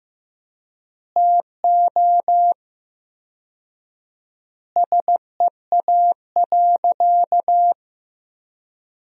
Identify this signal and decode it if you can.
Morse code, heard as a keyed tone, about 15 wpm, 700 Hz. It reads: TO SEA.
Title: 4x Word-Spacing - Sentences from Top 400 Words - 15wpm